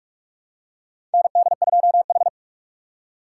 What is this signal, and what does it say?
Morse code, heard as a keyed tone, about 46 wpm, 700 Hz. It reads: ND2H
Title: USA Call Signs - 4 Characters - 6x Character Spacing - 45wpm